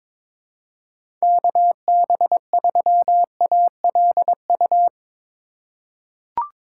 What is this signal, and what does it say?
Morse code, heard as a keyed tone, about 22 wpm, 700 Hz. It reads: KB3ALU
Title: USA Call Signs - 6 Characters - 4x Character Spacing - 22wpm